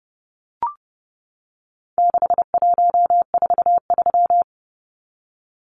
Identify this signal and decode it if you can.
Morse code, heard as a keyed tone, about 30 wpm, 700 Hz. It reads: 6143
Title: Numbers - 4-Digit 30wpm